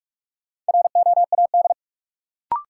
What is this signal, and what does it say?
Morse code, heard as a keyed tone, about 46 wpm, 700 Hz. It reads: ROAD
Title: Sets of 5 Words - New Word Drill - 45wpm